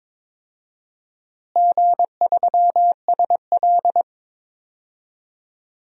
Morse code, heard as a keyed tone, about 22 wpm, 700 Hz. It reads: G3SL
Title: DXCC-100 Call Signs - 4 Characters - Repeated 2x - 22wpm